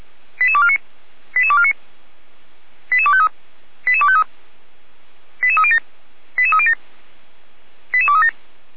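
Five-tone selective call (ZVEI), heard as quick runs of five tones, tones of about 70 ms. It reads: 88249 88249 88253 88253 88307 88307 88237
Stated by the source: Received on an SDR, ZVEI five-tone call 88392, from the feuerwehr-orlamuende.org audio archive (Feuerwehr Orlamünde).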